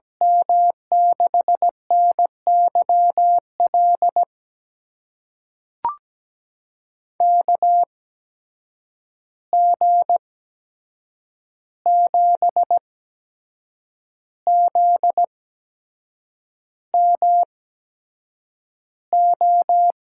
Morse code, heard as a keyed tone, about 17 wpm, spaced as for 7 wpm, 700 Hz. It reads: M6NYL KG7ZMO
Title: USA Call Signs - 6 Characters - 8x Character Spacing - 17wpm